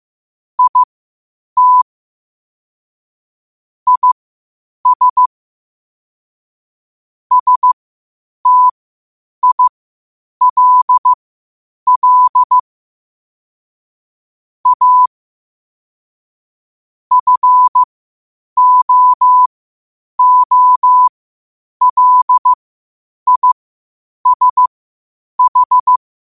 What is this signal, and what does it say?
Morse code, heard as a keyed tone, about 15 wpm, spaced as for 8 wpm, 1000 Hz. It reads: IT IS STILL A FOOLISH
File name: qotdfiles / 8wpm